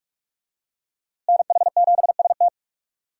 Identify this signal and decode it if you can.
Morse code, heard as a keyed tone, about 46 wpm, 700 Hz. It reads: NH7ST
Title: USA Call Signs - 5 Characters - Repeated 3x - 45wpm